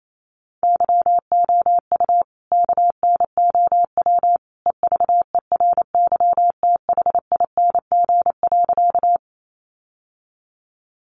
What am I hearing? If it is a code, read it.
Morse code, heard as a keyed tone, about 28 wpm, 700 Hz. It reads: YOU KNOW EVERYTHING.
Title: Sentences from Top 700 Words 28wpm